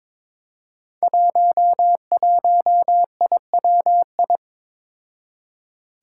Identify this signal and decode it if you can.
Morse code, heard as a keyed tone, about 22 wpm, 700 Hz. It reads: 11IWI
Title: Five Challenging Letter-Number Combinations 22wpm